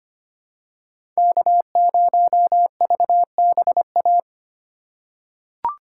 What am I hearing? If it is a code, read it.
Morse code, heard as a keyed tone, about 25 wpm, 700 Hz. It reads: K0VBA
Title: USA Call Signs - 5 Characters - 2x Character Spacing - 25wpm